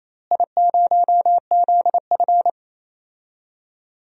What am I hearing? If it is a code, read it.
Morse code, heard as a keyed tone, about 27 wpm, 700 Hz. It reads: I0ZF
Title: DXCC-100 Call Signs - 4 Characters - Repeated 2x - 28wpm